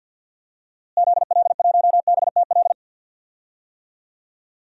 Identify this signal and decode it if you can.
Morse code, heard as a keyed tone, about 52 wpm, 700 Hz. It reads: ZL1BTL
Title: DXCC-100 Call Signs - 6 Characters - Repeated 3x - 50wpm